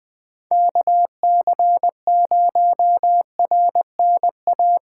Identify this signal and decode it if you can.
Morse code, heard as a keyed tone, about 20 wpm, 700 Hz. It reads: KC0RNA
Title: USA Callsign DE Callsign 20wpm